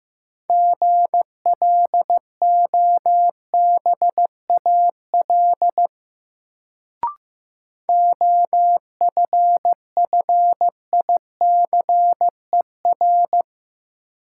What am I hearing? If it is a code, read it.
Morse code, heard as a keyed tone, about 15 wpm, 700 Hz. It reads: GLOBAL OFFICER
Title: Sentences from Top 1200 Words - New Word Drill - 15wpm